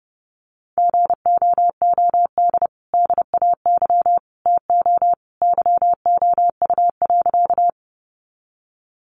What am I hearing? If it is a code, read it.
Morse code, heard as a keyed tone, about 30 wpm, 700 Hz. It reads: GOOD DAY TO YOU.
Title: Sentences from Top 200 Words 30wpm